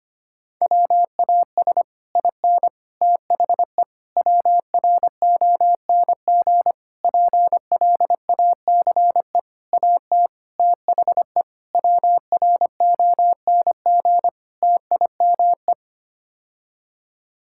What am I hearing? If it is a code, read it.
Morse code, heard as a keyed tone, about 25 wpm, 700 Hz. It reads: WAS IN THE WRONG PLACE AT THE WRONG TIME
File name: Sets of 9 Words - 25wpm